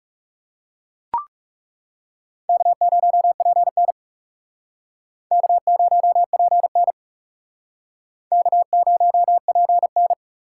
Morse code, heard as a keyed tone, about 37 wpm, 700 Hz. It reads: K0PN K0PN K0PN
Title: Speed Racing - USA 1x2 Call Signs 30wpm